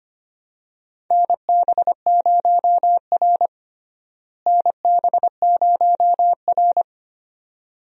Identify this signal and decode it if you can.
Morse code, heard as a keyed tone, about 25 wpm, 700 Hz. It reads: NB0R NB0R
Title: USA Call Signs - 4 Characters - Repeated 2x - 25wpm